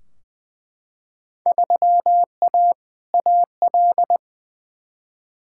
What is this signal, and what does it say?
Morse code, heard as a keyed tone, about 21 wpm, 700 Hz. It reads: 3A AL
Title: Field Day Exchanges - 20wpm